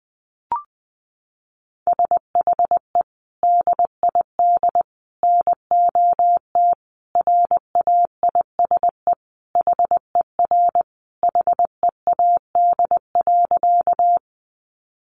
Morse code, heard as a keyed tone, about 20 wpm, 700 Hz. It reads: SHE DID NOT RAISE HER HEAD.
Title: Sentences from Top 1400 Words 20wpm